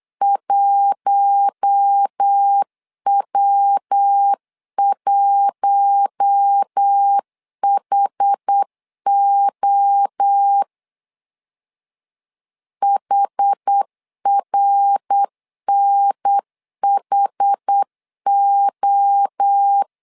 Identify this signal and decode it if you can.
Morse code, heard as a keyed tone, about 8 wpm, 800 Hz. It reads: 1W1HO HRNHO
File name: LEZIONI / 4 Q1 8wpm 3035